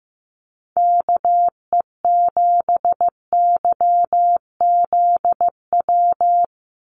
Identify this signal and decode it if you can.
Morse code, heard as a keyed tone, about 15 wpm, 700 Hz. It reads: KE7YZW
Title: USA Call Signs 15wpm